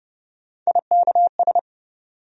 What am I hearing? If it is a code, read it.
Morse code, heard as a keyed tone, about 32 wpm, 700 Hz. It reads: IKS
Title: USA Call Sign Postfixes - 3 Letters 30wpm